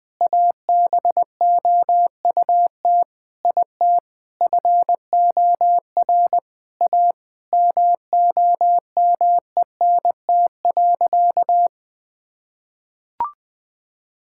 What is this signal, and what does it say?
Morse code, heard as a keyed tone, about 20 wpm, 700 Hz. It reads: ABOUT IT FOR A MOMENT.
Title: Sentences with 3-Word Phrases - 20wpm